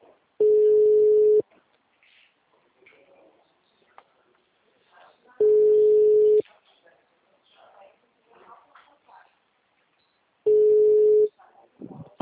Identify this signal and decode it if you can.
Ringback tone (Europe, 425 Hz), with speech.
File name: out- -2032- - - .613